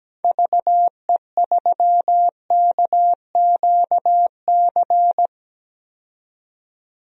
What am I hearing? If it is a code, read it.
Morse code, heard as a keyed tone, about 17 wpm, 700 Hz. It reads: VE3KQC
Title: DXCC-100 Call Signs - 6 Characters - 17wpm